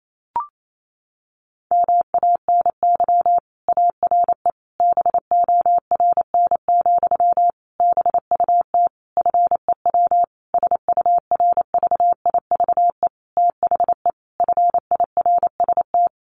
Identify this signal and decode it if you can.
Morse code, heard as a keyed tone, about 28 wpm, 700 Hz. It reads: MANY ARE BORN, BUT FEW SURVIVE THE FIRST
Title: Sentences from Top 1300 Words 28wpm